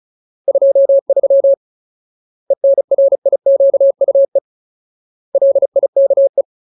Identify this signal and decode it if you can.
Morse code, heard as a keyed tone, about 34 wpm, 550 Hz. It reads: 23 ENRIQUE LIKE
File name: Ultimate Word List / 35WPM Ultimate